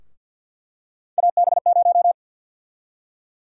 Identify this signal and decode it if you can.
Morse code, heard as a keyed tone, about 52 wpm, 700 Hz. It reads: AB0
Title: USA Call Sign Prefixes - Repeated 2x - 50wpm